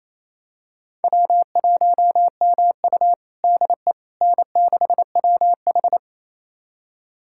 Morse code, heard as a keyed tone, about 28 wpm, 700 Hz. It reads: W1MU DE N6WH
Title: USA Callsign DE Callsign - 4 Characters - 28wpm